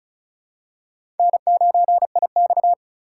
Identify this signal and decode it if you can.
Morse code, heard as a keyed tone, about 36 wpm, 700 Hz. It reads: N9IX
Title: USA Call Signs - 4x Character Spacing - 35wpm